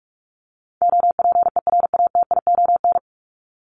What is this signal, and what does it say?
Morse code, heard as a keyed tone, about 44 wpm, 700 Hz. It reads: OPERATION
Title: Top 1501-1750 Words 45wpm